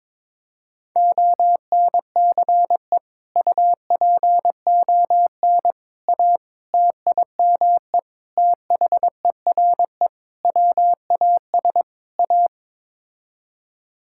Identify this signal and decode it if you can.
Morse code, heard as a keyed tone, about 22 wpm, 700 Hz. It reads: ONCE UPON A TIME THERE WAS A
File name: practice / 22wpm / Sets of 7 Words - 22wpm